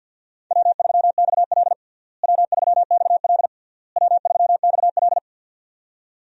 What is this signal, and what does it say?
Morse code, heard as a keyed tone, about 52 wpm, 700 Hz. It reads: W3XL W3XL W3XL
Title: USA Call Signs - 4 Characters - Repeated 3x - 50wpm